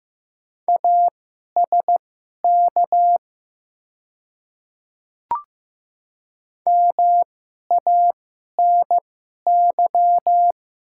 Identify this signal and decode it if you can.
Morse code, heard as a keyed tone, about 15 wpm, spaced as for 11 wpm, 690 Hz. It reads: ASK MANY
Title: 2x Character Spacing - Top 200 Words - 15wpm